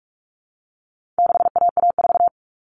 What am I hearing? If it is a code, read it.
Morse code, heard as a keyed tone, about 44 wpm, 700 Hz. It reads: 6AA4
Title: All Cut Numbers - 4-Digit 45wpm